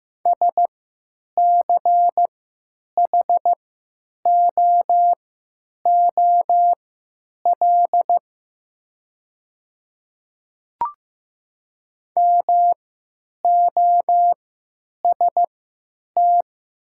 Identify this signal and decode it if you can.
Morse code, heard as a keyed tone, about 15 wpm, spaced as for 9 wpm, 700 Hz. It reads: SCHOOL MOST
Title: 3x Character Spacing - Top 300 Words - 15wpm